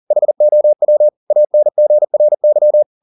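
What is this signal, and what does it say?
Morse code, heard as a keyed tone, about 40 wpm, 600 Hz. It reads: HOW ANGRY
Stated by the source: Morse code at 40 wpm, practice text, CW med Gnister.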